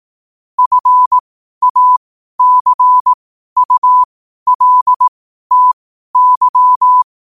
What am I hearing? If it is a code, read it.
Morse code, heard as a keyed tone, about 18 wpm, spaced as for 12 wpm, 1000 Hz. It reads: FACULTY
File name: qotdfiles / 12wpm